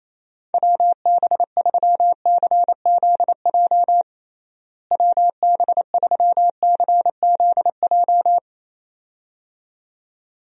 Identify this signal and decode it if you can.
Morse code, heard as a keyed tone, about 28 wpm, 700 Hz. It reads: WB3CZJ WB3CZJ
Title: USA Call Signs - 6 Characters - Repeated 2x - 28wpm